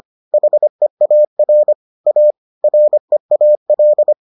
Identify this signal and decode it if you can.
Morse code, heard as a keyed tone, about 25 wpm, 600 Hz. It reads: HEAR A REAL